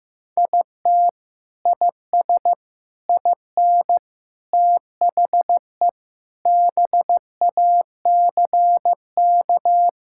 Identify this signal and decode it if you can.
Morse code, heard as a keyed tone, about 15 wpm, 700 Hz. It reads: IT IS IN THE BACK
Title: Sentences from Top 1000 Words - Review Part 5 - 15wpm